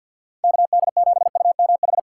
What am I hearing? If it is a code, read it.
Morse code, heard as a keyed tone, about 48 wpm, 700 Hz. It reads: KD7UMH
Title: USA Call Signs - 6 Characters - 6x Character Spacing - 50wpm